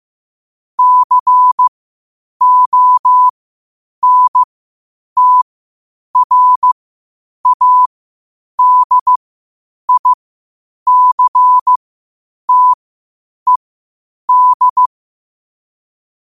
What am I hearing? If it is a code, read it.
Morse code, heard as a keyed tone, about 15 wpm, spaced as for 8 wpm, 1000 Hz. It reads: CONTRADICTED
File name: qotdfiles / 8wpm